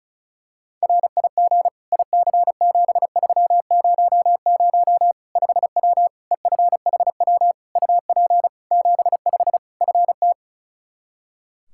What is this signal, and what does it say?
Morse code, heard as a keyed tone, about 34 wpm, 700 Hz. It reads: RIG IC7300 5W EFHW UP 75 FT
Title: Practice QSO - Exchange 2 - Info - 35wpm